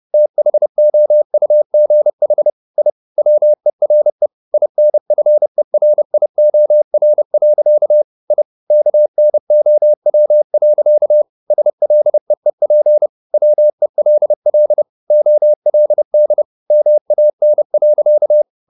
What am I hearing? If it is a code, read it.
Morse code, heard as a keyed tone, about 30 wpm, 600 Hz. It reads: THOUGH I WERE INFERIOR. I KNOW. SLEEP WELL OLD MAN.
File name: the old man and the sea / the old man and the sea 1 30wpm